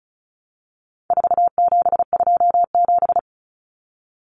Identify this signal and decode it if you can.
Morse code, heard as a keyed tone, about 34 wpm, 700 Hz. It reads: 4727
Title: Numbers - 4-Digit 35wpm